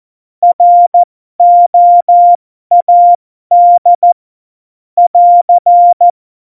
Morse code, heard as a keyed tone, about 14 wpm, spaced as for 12 wpm, 700 Hz. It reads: ROAD <AR>